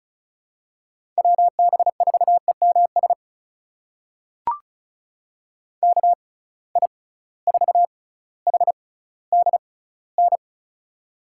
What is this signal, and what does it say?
Morse code, heard as a keyed tone, about 36 wpm, spaced as for 18 wpm, 700 Hz. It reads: WB4EMS KI4HDN
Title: USA Call Signs - 6 Characters - 6x Character Spacing - 35wpm